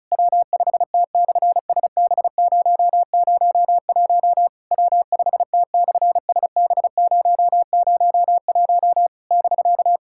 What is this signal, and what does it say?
Morse code, heard as a keyed tone, about 36 wpm, 700 Hz. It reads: W5T/SB001 W5T/SB001 <BK>
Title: SOTA QSO Practice 35wpm